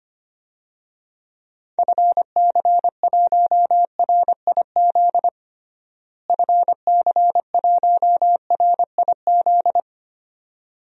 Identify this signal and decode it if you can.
Morse code, heard as a keyed tone, about 25 wpm, 700 Hz. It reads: FC1RIZ FC1RIZ
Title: DXCC-100 Call Signs - 6 Characters - Repeated 2x - 25wpm